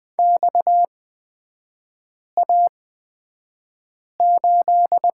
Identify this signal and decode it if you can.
Morse code, heard as a keyed tone, about 20 wpm, spaced as for 5 wpm, 690 Hz. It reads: XA8
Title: 229 Lesson 37e - Letter x - Three Characters 20wpm